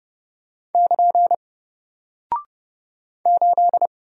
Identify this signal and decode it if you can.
Morse code, heard as a keyed tone, about 30 wpm, 700 Hz. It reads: <KN> 8
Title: Single Character (Letter:Number:Punctuation:Prosign) 30wpm